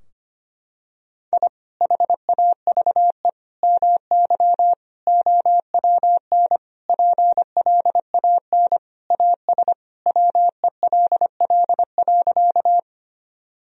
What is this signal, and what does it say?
Morse code, heard as a keyed tone, about 25 wpm, 700 Hz. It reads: I HAVE MY OWN PLAN AS WELL.
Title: Sentences from Top 1000 Words - Review Part 2 - 25wpm